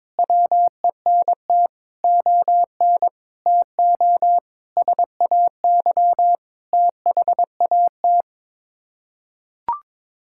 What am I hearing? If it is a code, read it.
Morse code, heard as a keyed tone, about 21 wpm, 700 Hz. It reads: WENT ON TO SAY THAT
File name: practice / 22wpm / Sets of 6 Words - 22wpm